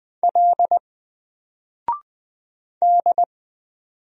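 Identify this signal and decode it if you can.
Morse code, heard as a keyed tone, about 20 wpm, 700 Hz. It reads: L D
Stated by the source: Single letter and number Morse drill at 20 wpm, from N4LCD.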